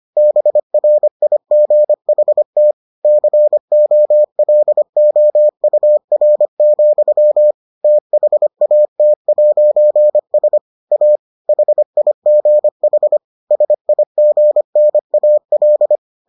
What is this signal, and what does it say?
Morse code, heard as a keyed tone, about 25 wpm, 600 Hz. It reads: BRIGHT COLOUR, THAT'S A HIGH SIGNAL